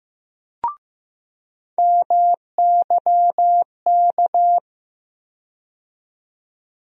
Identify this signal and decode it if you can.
Morse code, heard as a keyed tone, about 15 wpm, 700 Hz. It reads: MYK
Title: USA Call Sign Postfixes - 3 Letters 15wpm